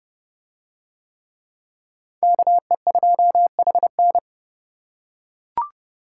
Morse code, heard as a keyed tone, about 27 wpm, 700 Hz. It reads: KE2HN